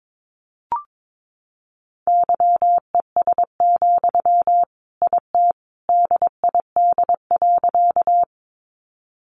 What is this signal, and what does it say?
Morse code, heard as a keyed tone, about 22 wpm, 700 Hz. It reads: YES, IT DID.
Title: Sentences from Top 500 Words 22wpm